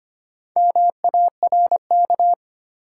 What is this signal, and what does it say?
Morse code, heard as a keyed tone, about 25 wpm, 700 Hz. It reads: MARK
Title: Sentences from Top 300 Words - New Word Drill - 25wpm